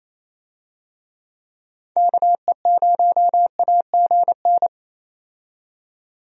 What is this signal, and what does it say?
Morse code, heard as a keyed tone, about 28 wpm, 700 Hz. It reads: KE0AGN
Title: USA Call Signs - 6 Characters - 28wpm